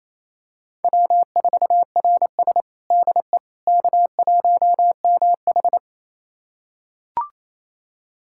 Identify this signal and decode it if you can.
Morse code, heard as a keyed tone, about 29 wpm, 690 Hz. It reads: W4RS DE K1MH
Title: USA Callsign DE Callsign - 4 Characters - 28wpm